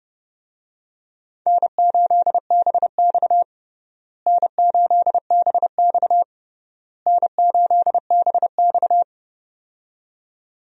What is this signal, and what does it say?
Morse code, heard as a keyed tone, about 30 wpm, 700 Hz. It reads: N8BX N8BX N8BX
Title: USA Call Signs - 4 Characters - Repeated 3x - 30wpm